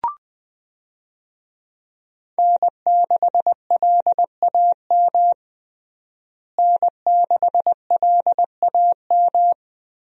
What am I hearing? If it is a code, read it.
Morse code, heard as a keyed tone, about 21 wpm, 700 Hz. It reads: N6LAM N6LAM
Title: USA Call Signs - 5 Characters - Repeated 3x - 20wpm